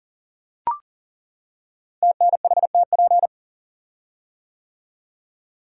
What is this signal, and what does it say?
Morse code, heard as a keyed tone, about 40 wpm, 700 Hz. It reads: TNHTP
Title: Code Groups - 5 Letters - 40wpm